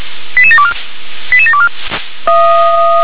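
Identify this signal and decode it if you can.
Five-tone selective call (ZVEI), heard as quick runs of five tones, tones of about 70 ms. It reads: 88724 88724
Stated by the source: ZVEI five-tone call received by SDR.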